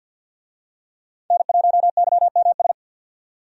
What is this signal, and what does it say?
Morse code, heard as a keyed tone, about 49 wpm, 700 Hz. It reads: N1YMS
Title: USA Call Signs - 5 Characters - 50wpm